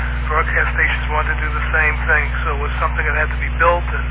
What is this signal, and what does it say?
Emergency Alert System (SAME) tones, heard as harsh buzzing bursts, with speech. It is heard from 0.5 to 3.5 s.